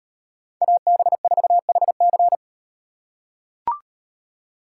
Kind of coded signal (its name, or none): Morse code